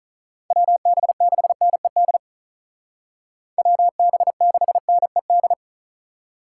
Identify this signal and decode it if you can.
Morse code, heard as a keyed tone, about 44 wpm, 700 Hz. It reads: WB6NED WB6NED
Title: Speed Racing - USA 2x3 Call Signs 35wpm